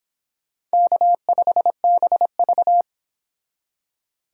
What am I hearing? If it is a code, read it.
Morse code, heard as a keyed tone, about 32 wpm, 700 Hz. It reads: K5BV